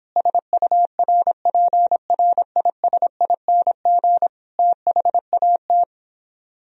Morse code, heard as a keyed tone, about 29 wpm, 700 Hz. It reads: SURPRISING THAT